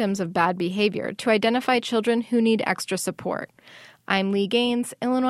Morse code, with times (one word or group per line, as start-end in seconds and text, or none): none